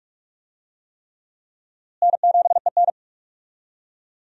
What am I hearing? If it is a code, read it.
Morse code, heard as a keyed tone, about 46 wpm, 700 Hz. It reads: N7EN
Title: USA Call Signs - 4 Characters - Repeated 3x - 45wpm